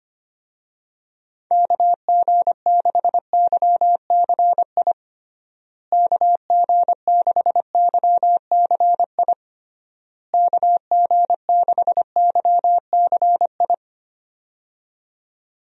Morse code, heard as a keyed tone, about 25 wpm, 700 Hz. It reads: KG6YCI KG6YCI KG6YCI
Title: USA Call Signs - 6 Characters - Repeated 3x - 25wpm